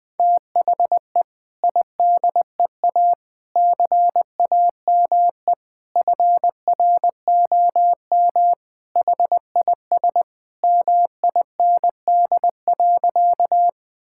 Morse code, heard as a keyed tone, about 20 wpm, 700 Hz. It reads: THE IDEA CAME FROM HIS MIND.
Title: Sentences with 2-Word Phrases - 20wpm